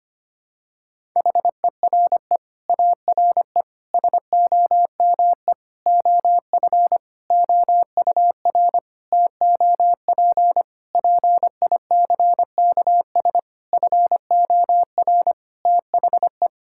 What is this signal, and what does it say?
Morse code, heard as a keyed tone, about 25 wpm, 700 Hz. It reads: HERE ARE SOME OF OUR TOP PICKS FOR THE
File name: Sets of 9 Words - 25wpm